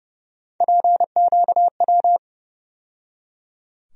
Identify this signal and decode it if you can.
Morse code, heard as a keyed tone, about 30 wpm, 700 Hz. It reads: PQW
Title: USA Call Sign Postfixes - 3 Letters 30wpm